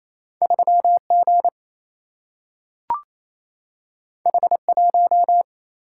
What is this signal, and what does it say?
Morse code, heard as a keyed tone, about 28 wpm, 700 Hz. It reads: 3 G H 1
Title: Single-Letter Single-Number 28wpm